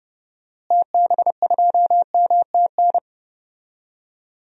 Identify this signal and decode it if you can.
Morse code, heard as a keyed tone, about 32 wpm, 700 Hz. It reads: TB2MTN
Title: DXCC-100 Call Signs - 6 Characters - Repeated 2x - 30wpm